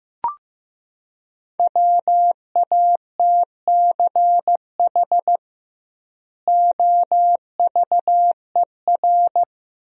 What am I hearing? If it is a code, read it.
Morse code, heard as a keyed tone, about 16 wpm, 700 Hz. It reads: WATCH OVER
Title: 2x Word-Spacing - Sentences from Top 300 Words - 15wpm